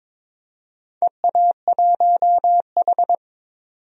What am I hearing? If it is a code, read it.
Morse code, heard as a keyed tone, about 22 wpm, 700 Hz. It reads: EA1H
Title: DXCC-100 Call Signs - 4 Characters - 22wpm